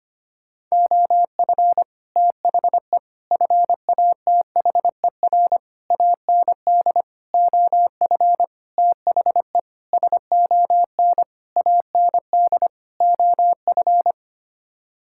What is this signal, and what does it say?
Morse code, heard as a keyed tone, about 25 wpm, 700 Hz. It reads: OF THE FATHER AND OF THE SON AND OF
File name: Sets of 9 Words - 25wpm